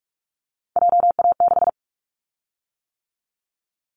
Morse code, heard as a keyed tone, about 44 wpm, 700 Hz. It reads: JA6
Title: Callsign Prefixes for Top 25 DXCC - 45wpm